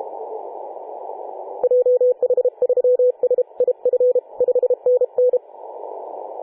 Morse code, heard as a keyed tone, about 32 wpm, 500 Hz. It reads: JH3SIF 5NN
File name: 2013/Sep/21 2035z 3D2GC 18.072MHz CW